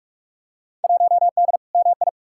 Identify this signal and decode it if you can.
Morse code, heard as a keyed tone, about 47 wpm, 700 Hz. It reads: 1D MI
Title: Field Day Exchanges - 45wpm